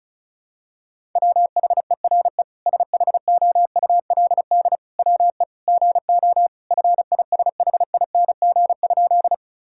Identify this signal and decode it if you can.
Morse code, heard as a keyed tone, about 34 wpm, 700 Hz. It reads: WHERE SHOULD WE GO FISHING?
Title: Sentences from Top 1000 Words - Review Part 5 - 35wpm